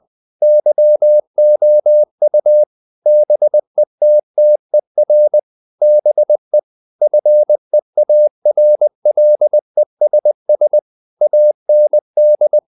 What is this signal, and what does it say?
Morse code, heard as a keyed tone, about 20 wpm, 600 Hz. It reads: YOU BETTER BE FEARLESS AND
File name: the old man and the sea part 2 20wpm